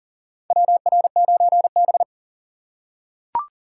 Morse code, heard as a keyed tone, about 40 wpm, 700 Hz. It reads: WR9B
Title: USA Call Signs - 4 Characters - 2x Character Spacing - 40wpm